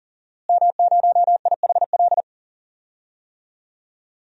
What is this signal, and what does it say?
Morse code, heard as a keyed tone, about 37 wpm, 700 Hz. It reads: M0IHL
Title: DXCC-100 Call Signs - 5 Characters - Repeated 3x - 40wpm